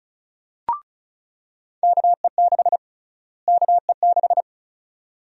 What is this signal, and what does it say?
Morse code, heard as a keyed tone, about 34 wpm, 700 Hz. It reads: KE6 KE6
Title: USA Call Sign Prefixes - Repeated 2x - 35wpm